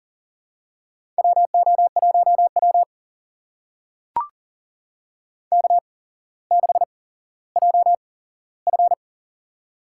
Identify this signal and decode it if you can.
Morse code, heard as a keyed tone, about 37 wpm, spaced as for 17 wpm, 700 Hz. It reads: WO1W K6JF
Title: USA Call Signs - 4 Characters - 8x Character Spacing - 40wpm